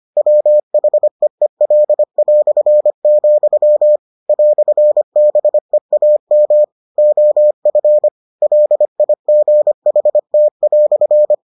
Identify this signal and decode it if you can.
Morse code, heard as a keyed tone, about 25 wpm, 600 Hz. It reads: WHEEL", "BEAM OF LIGHT"